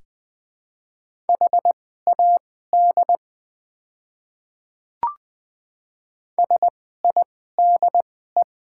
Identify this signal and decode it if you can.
Morse code, heard as a keyed tone, about 21 wpm, spaced as for 15 wpm, 690 Hz. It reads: HAD SIDE
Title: 2x Character Spacing - Top 100 Words - 20wpm